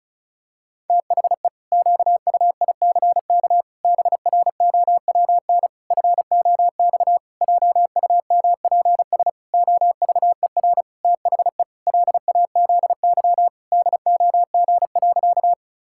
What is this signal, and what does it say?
Morse code, heard as a keyed tone, about 36 wpm, 700 Hz. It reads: THE QUICK BROWN FOX JUMPS OVER THE LAZY DOG.
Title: Pangram Challenge v2 - 35wpm